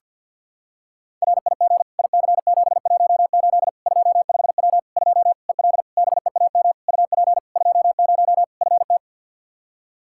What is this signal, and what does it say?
Morse code, heard as a keyed tone, about 47 wpm, 700 Hz. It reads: RIG IC718 25W 2 EL BEAM UP 20 FT
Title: Practice QSO - Exchange 2 - Info - 50wpm